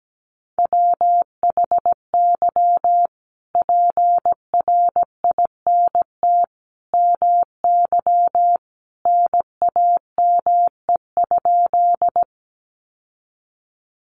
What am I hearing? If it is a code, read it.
Morse code, heard as a keyed tone, about 17 wpm, 700 Hz. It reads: WHY PRINT MY NAME?
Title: Sentences from Top 1400 Words 17wpm